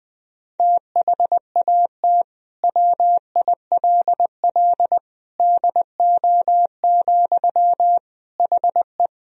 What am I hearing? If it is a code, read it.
Morse code, heard as a keyed tone, about 20 wpm, 700 Hz. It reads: THAT WILL DO, HE
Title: Sentences from Top 100 Words and Abbreviations 20wpm